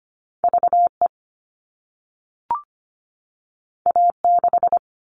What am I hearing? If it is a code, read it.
Morse code, heard as a keyed tone, about 25 wpm, 700 Hz. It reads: VE A6
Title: All Cut Numbers - 2-Digit 25wpm